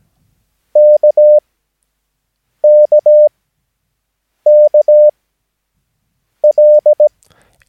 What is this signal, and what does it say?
Morse code, heard as a keyed tone, about 17 wpm, spaced as for 6 wpm, 600 Hz. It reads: KKKL
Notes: Övningsmaterial med alfabetet A till Ö samt siffror 0 till 9 enligt International/Continental Morse Code.